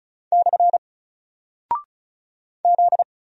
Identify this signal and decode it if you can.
Morse code, heard as a keyed tone, about 34 wpm, 700 Hz. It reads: / Z
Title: Single Character (Letter:Number:Punctuation:Prosign) Alt 35wpm